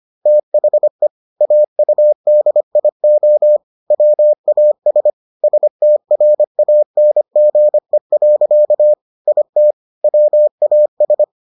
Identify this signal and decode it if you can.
Morse code, heard as a keyed tone, about 25 wpm, 600 Hz. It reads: THE AUDIO WAS STRANGE. IT WAS